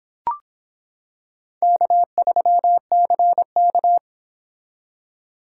Morse code, heard as a keyed tone, about 25 wpm, 700 Hz. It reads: K3CK